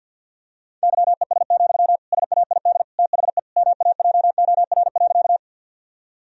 Morse code, heard as a keyed tone, about 50 wpm, 700 Hz. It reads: YES, SAID THE MAJOR.
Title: Sentences from Top 1000 Words - Review Part 4 - 50wpm